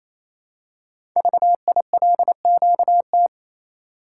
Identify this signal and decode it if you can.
Morse code, heard as a keyed tone, about 27 wpm, 700 Hz. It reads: VILQT
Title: Code Groups - 5 Letters - 28wpm